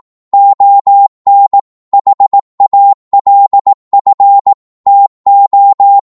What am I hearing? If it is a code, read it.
Morse code, heard as a keyed tone, about 18 wpm, 800 Hz. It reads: ON HALF TO
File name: Morse / psg-18wpm